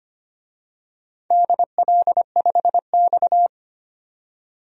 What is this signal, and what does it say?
Morse code, heard as a keyed tone, about 25 wpm, 700 Hz. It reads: DL5X
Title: DXCC-100 Call Signs - 4 Characters - 25wpm